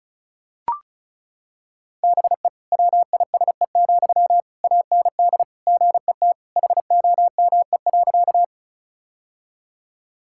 Morse code, heard as a keyed tone, about 36 wpm, 700 Hz. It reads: BE WISE, AND GET HOME.
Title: Sentences from Top 1600 Words 35wpm